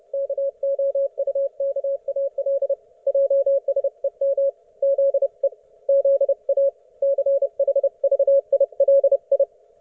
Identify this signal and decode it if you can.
Morse code, heard as a keyed tone, about 30 wpm, 550 Hz. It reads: KOUKAL JSEM ZE ZA CHVILI
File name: Archiv / CUC181 / ok1fhi cul181